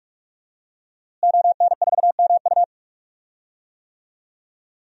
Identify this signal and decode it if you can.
Morse code, heard as a keyed tone, about 43 wpm, 700 Hz. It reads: ON4MU
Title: DXCC-100 Call Signs - 5 Characters - Repeated 3x - 45wpm